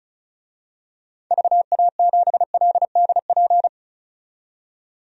Morse code, heard as a keyed tone, about 34 wpm, 700 Hz. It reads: VA7LDP
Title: DXCC-100 Call Signs - 6 Characters - 35wpm